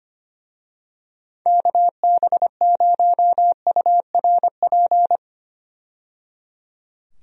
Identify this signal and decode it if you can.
Morse code, heard as a keyed tone, about 25 wpm, 700 Hz. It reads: KB0URP
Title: USA Call Signs - 6 Characters - 25wpm